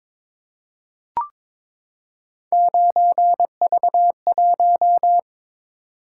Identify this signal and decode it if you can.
Morse code, heard as a keyed tone, about 22 wpm, 700 Hz. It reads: 9V1
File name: practice / 22wpm / Three Letter-Number Combinations - 22wpm